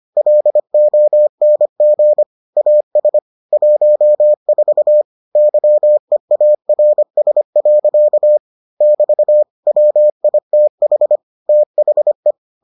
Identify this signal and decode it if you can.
Morse code, heard as a keyed tone, about 25 wpm, 600 Hz. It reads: LONG AS 14 YEARS. <BT> WITH THE